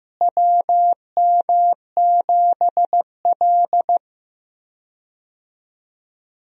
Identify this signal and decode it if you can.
Morse code, heard as a keyed tone, about 15 wpm, 700 Hz. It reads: WM7L
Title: USA Call Signs - 4 Characters - Repeated 3x - 15wpm